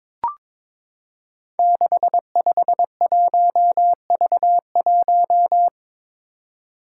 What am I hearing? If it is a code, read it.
Morse code, heard as a keyed tone, about 22 wpm, 700 Hz. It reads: 651V1
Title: Five Challenging Letter-Number Combinations 22wpm